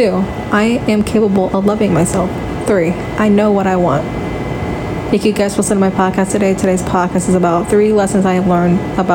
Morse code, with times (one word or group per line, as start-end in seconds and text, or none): none